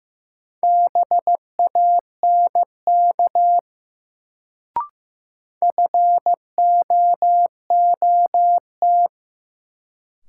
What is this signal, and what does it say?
Morse code, heard as a keyed tone, about 16 wpm, 700 Hz. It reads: BANK FOOT
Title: Sets of 5 Words - New Word Drill - 15wpm